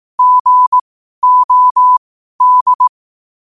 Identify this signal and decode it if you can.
Morse code, heard as a keyed tone, about 18 wpm, spaced as for 12 wpm, 1000 Hz. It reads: GOD
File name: qotdfiles / 12wpm